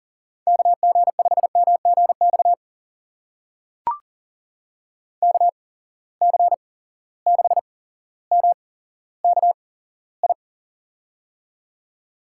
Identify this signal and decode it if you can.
Morse code, heard as a keyed tone, about 40 wpm, spaced as for 17 wpm, 700 Hz. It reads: KG5MGX KC6MKI
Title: USA Call Signs - 6 Characters - 8x Character Spacing - 40wpm